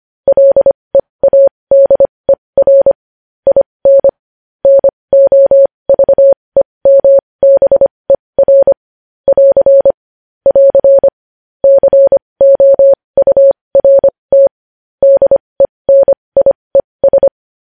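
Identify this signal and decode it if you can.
Morse code, heard as a keyed tone, about 25 wpm, spaced as for 20 wpm, 550 Hz. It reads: LEADER IN NOVEMBER <AR> <AR> COURT DENIES
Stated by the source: Morse code news headlines practice at 20 wpm.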